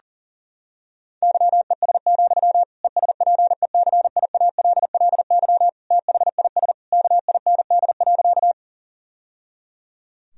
Morse code, heard as a keyed tone, about 40 wpm, 700 Hz. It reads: YES, ESPECIALLY THIS KIND.
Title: Sentences from Top 1000 Words - Review Part 4 - 40wpm